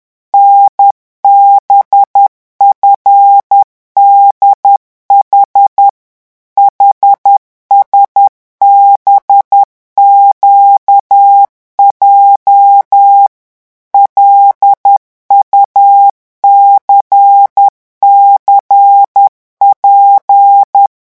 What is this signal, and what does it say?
Morse code, heard as a keyed tone, about 10 wpm, 800 Hz. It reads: NBFDH HSBQJ LUCCP